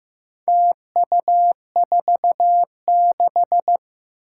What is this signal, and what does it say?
Morse code, heard as a keyed tone, about 15 wpm, 700 Hz. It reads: TU46
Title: Four Challenging Letter-Number Combinations 15wpm